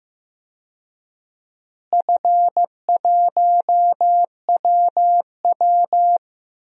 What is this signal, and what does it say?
Morse code, heard as a keyed tone, about 15 wpm, 700 Hz. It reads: F1WW
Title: DXCC-100 Call Signs - 4 Characters - Repeated 2x - 15wpm